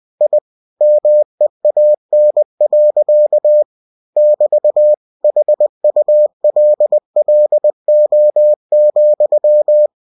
Morse code, heard as a keyed tone, about 20 wpm, 600 Hz. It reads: I MEAN. <BT> HULLO,
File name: cw-books / winnie-the-pooh / winnie-20wpm-page-0088